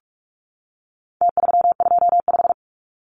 Morse code, heard as a keyed tone, about 44 wpm, 700 Hz. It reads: T325
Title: Numbers - Contest Style - 4-Digit 45wpm